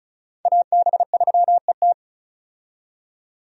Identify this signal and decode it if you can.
Morse code, heard as a keyed tone, about 34 wpm, 700 Hz. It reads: AB3ET
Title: USA Call Signs - 5 Characters - 35wpm